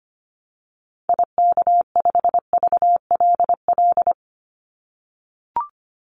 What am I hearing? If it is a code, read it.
Morse code, heard as a keyed tone, about 25 wpm, 700 Hz. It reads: IK5VLL
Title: Call Signs - 25wpm